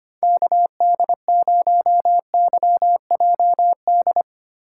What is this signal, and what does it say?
Morse code, heard as a keyed tone, about 25 wpm, 700 Hz. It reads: KD0YJD
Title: USA Call Signs - 6 Characters - 3x Character Spacing - 25wpm